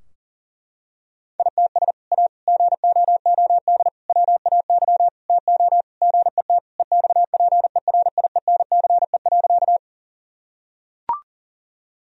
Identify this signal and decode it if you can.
Morse code, heard as a keyed tone, about 37 wpm, 700 Hz. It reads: ITS A GOOD WAY TO GET EXPERIENCE.
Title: Sentences from Top 1000 Words - Review Part 3 - 40wpm